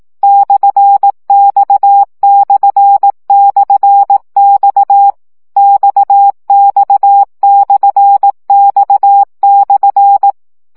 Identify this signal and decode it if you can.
Morse code, heard as a keyed tone, about 18 wpm, 800 Hz. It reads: /X//X XX/X/
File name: CW gyakorlas x per 15 baud